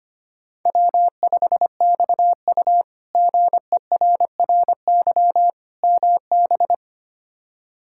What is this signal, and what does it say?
Morse code, heard as a keyed tone, about 25 wpm, 700 Hz. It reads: W5XU GERRY MB
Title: K1USN SST QSO Practice - 25wpm